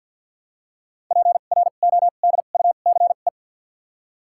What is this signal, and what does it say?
Morse code, heard as a keyed tone, about 49 wpm, spaced as for 36 wpm, 700 Hz. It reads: PRODUCE
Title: 2x Character Spacing - Top 500 Words - 50wpm